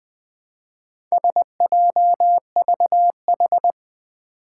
Morse code, heard as a keyed tone, about 20 wpm, 700 Hz. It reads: SJVH
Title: Four Challenging Letter-Number Combinations 20wpm